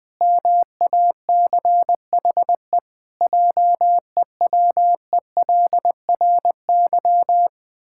Morse code, heard as a keyed tone, about 20 wpm, 700 Hz. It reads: MACHE JEWELRY
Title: Pangram Challenge v2 - 20wpm